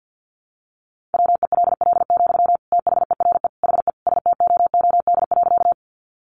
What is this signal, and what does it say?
Morse code, heard as a keyed tone, about 50 wpm, 700 Hz. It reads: WELL, THERE HE STOOD.
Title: Sentences from Top 500 Words 50wpm